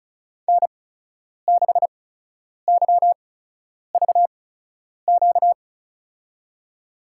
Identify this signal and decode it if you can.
Morse code, heard as a keyed tone, about 36 wpm, spaced as for 10 wpm, 700 Hz. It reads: N6YVQ
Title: USA Call Signs - 5 Characters - 8x Character Spacing - 35wpm